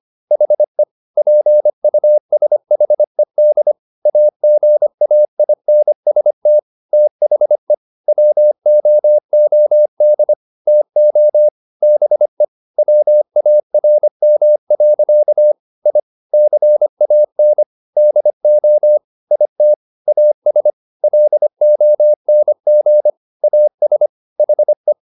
Morse code, heard as a keyed tone, about 25 wpm, 600 Hz. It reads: HE PUSHED AGAINST THE WOOD TO BE WARM. I CAN DO IT AS LONG AS HE
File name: the old man and the sea part 1 25wpm